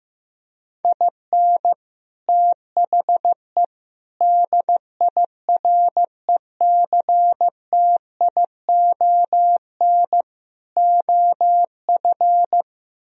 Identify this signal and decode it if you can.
Morse code, heard as a keyed tone, about 15 wpm, 700 Hz. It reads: IN THE DIRECTION OF
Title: Speed-Racing - Sets of 4 Words 15wpm